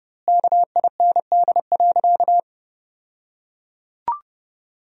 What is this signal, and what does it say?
Morse code, heard as a keyed tone, about 32 wpm, 700 Hz. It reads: KIND.
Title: 2x Word-Spacing - Sentences from Top 200 Words - 30wpm